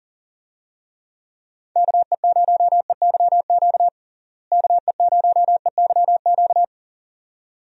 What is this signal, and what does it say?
Morse code, heard as a keyed tone, about 40 wpm, 700 Hz. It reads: KE0EYQ KE0EYQ
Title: USA Call Signs - 6 Characters - Repeated 2x - 40wpm